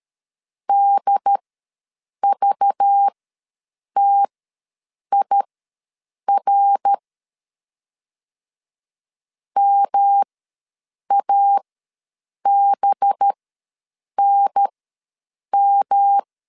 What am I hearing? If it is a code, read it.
Morse code, heard as a keyed tone, about 13 wpm, spaced as for 7 wpm, 800 Hz. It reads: DVTIR MABNM
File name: LEZIONI / 5 LBVC / 5 LBV ws 12wpm 3015